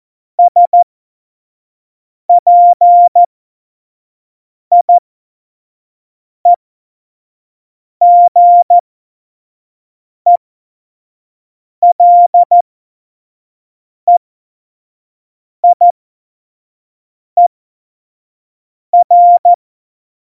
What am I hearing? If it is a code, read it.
Morse code, heard as a keyed tone, about 14 wpm, spaced as for 5 wpm, 700 Hz. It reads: SPIEGELEIER
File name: Morsekurs / Witze / Witz 17. September 2019, 25 F70